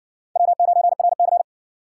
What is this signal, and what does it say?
Morse code, heard as a keyed tone, about 60 wpm, 700 Hz. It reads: W9RC